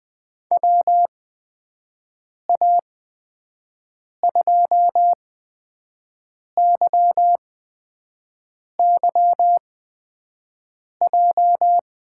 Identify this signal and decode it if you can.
Morse code, heard as a keyed tone, about 20 wpm, spaced as for 5 wpm, 700 Hz. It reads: WA2YYJ
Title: USA Call Signs - 6 Characters - 8x Character Spacing - 20wpm